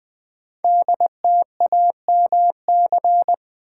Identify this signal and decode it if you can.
Morse code, heard as a keyed tone, about 20 wpm, 700 Hz. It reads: DTAMC
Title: Code Groups - 5 Letters - 20wpm